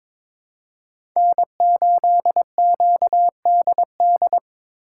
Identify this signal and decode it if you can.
Morse code, heard as a keyed tone, about 22 wpm, 700 Hz. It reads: N8QDD
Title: USA Call Signs - 2x Character Spacing - 22wpm